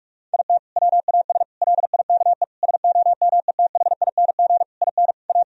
Morse code, heard as a keyed tone, about 41 wpm, 700 Hz. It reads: IT WAS LIKE SOMETHING IN A
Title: Sentences from Top 1000 Words - Review Part 1 - 45wpm